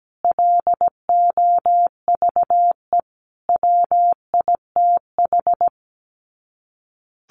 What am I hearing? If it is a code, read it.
Morse code, heard as a keyed tone, about 17 wpm, 700 Hz. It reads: LOVE WITH
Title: Sets of 5 Words Encore 17wpm